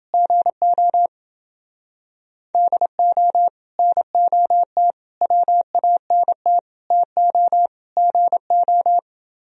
Morse code, heard as a keyed tone, about 28 wpm, 700 Hz. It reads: GO DO NOT WANT TO GO